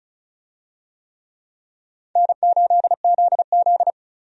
Morse code, heard as a keyed tone, about 36 wpm, 700 Hz. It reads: N8ZZ
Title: USA Call Signs - 4 Characters - Repeated 2x - 35wpm